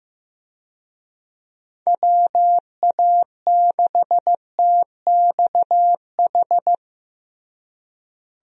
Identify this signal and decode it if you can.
Morse code, heard as a keyed tone, about 16 wpm, 700 Hz. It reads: WA6TXH
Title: USA Call Signs - 6 Characters - Repeated 2x - 15wpm